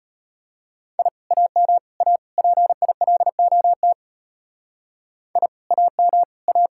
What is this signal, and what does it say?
Morse code, heard as a keyed tone, about 31 wpm, 700 Hz. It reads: I AM A PILOT I AM A
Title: Speed-Racing Common QSO Phrases Encore - 25wpm